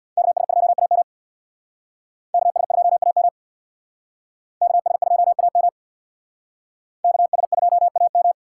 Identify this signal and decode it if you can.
Morse code, heard as a keyed tone, about 62 wpm, 700 Hz. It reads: KS2AM KS2AM KS2AM KS2AM
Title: Speed Racing - USA Call Signs 50wpm